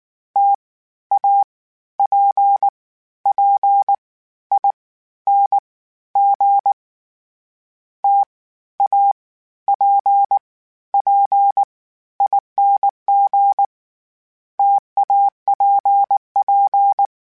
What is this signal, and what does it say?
Morse code, heard as a keyed tone, about 19 wpm, 800 Hz. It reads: T A P P I N G T A P P ING TAPP
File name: Variable Spacing / Variable Spacing-mp3s / T AOEs @18